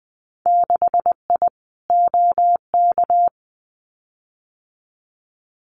Morse code, heard as a keyed tone, about 20 wpm, 700 Hz. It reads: 6I OK
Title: Winter Field Day Exchanges Encore 20wpm